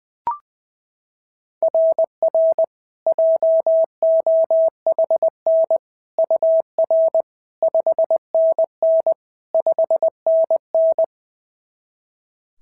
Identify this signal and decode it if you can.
Morse code, heard as a keyed tone, about 20 wpm, 690 Hz. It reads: RR JOHN UR 5NN 5NN
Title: SOTA QSO Practice 20wpm